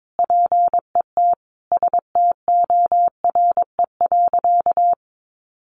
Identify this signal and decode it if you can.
Morse code, heard as a keyed tone, about 22 wpm, 700 Hz. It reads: PET STORE.
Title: Easy Phrases 22wpm